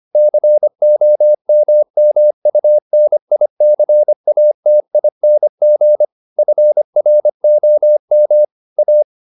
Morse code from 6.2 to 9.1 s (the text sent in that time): FROM A